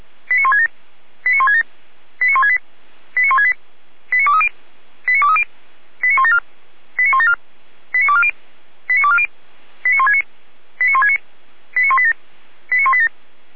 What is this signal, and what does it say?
Five-tone selective call (ZVEI), heard as quick runs of five tones, tones of about 70 ms. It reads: 89167 89167 89168 89168 89230 89230 89174 89174 89240 89240 89169 89169 89187 89187
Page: ZVEI